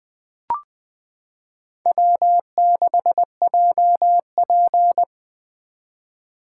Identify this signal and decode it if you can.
Morse code, heard as a keyed tone, about 20 wpm, 700 Hz. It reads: W6JP
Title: 2023 SOTA Chaser Call Signs - 20wpm